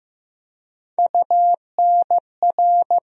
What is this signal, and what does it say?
Morse code, heard as a keyed tone, about 15 wpm, 700 Hz. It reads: UNR
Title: USA Call Sign Postfixes - 3 Letters 15wpm